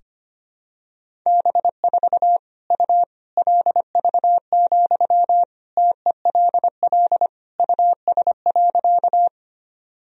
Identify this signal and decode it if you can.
Morse code, heard as a keyed tone, about 25 wpm, 700 Hz. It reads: B4 U LV, TELL US.
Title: Sentences from Top 400 Words and Abbreviations 25wpm